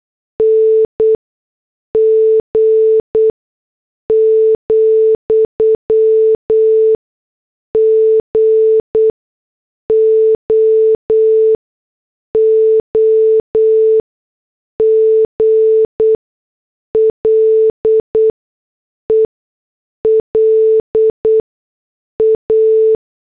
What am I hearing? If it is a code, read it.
Morse code, heard as a keyed tone, about 8 wpm, spaced as for 6 wpm, 440 Hz. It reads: NG, GOOGLE LA